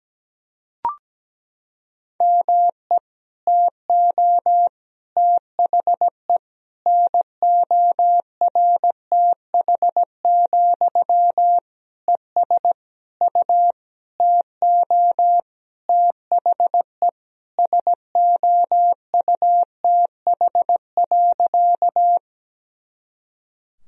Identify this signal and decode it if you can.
Morse code, heard as a keyed tone, about 17 wpm, 700 Hz. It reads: ME TO THE NORTH, ES U TO THE SOUTH.
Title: Sentences from Top 400 Words and Abbreviations 17wpm